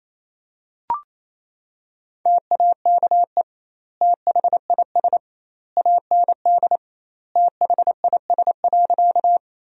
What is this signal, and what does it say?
Morse code, heard as a keyed tone, about 28 wpm, 700 Hz. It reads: TAKE THIS AND THIS.
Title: 2x Word-Spacing - Sentences from Top 200 Words - 28wpm